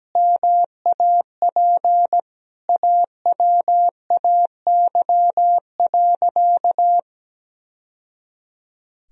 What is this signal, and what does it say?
Morse code, heard as a keyed tone, about 17 wpm, 700 Hz. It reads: MAP AWAY.
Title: Sentences from Top 1000 Words - Review Part 2 - 17wpm